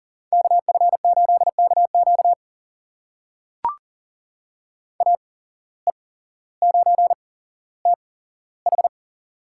Morse code, heard as a keyed tone, about 40 wpm, spaced as for 17 wpm, 700 Hz. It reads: KF8KQ AE9TH
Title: USA Call Signs - 5 Characters - 8x Character Spacing - 40wpm